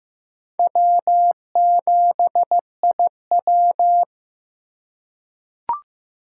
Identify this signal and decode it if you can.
Morse code, heard as a keyed tone, about 15 wpm, 700 Hz. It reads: W7IW
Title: USA Call Signs - 3x Character Spacing - 15wpm